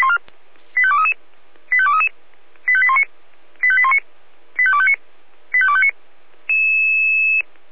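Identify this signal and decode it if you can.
Five-tone selective call (ZVEI), heard as quick runs of five tones, tones of about 70 ms. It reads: 85233 85233 85819 85819 85259 85259
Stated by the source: Five-tone ZVEI selective call received by SDR.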